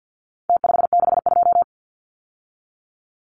Morse code, heard as a keyed tone, about 50 wpm, 700 Hz. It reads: T562
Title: Numbers - Contest Style - 4-Digit 50wpm